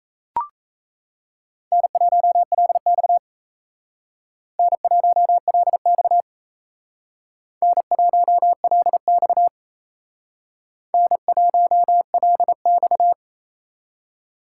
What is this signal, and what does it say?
Morse code, heard as a keyed tone, about 35 wpm, 700 Hz. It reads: N1LX N1LX N1LX N1LX
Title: Speed Racing - USA 1x2 Call Signs 28wpm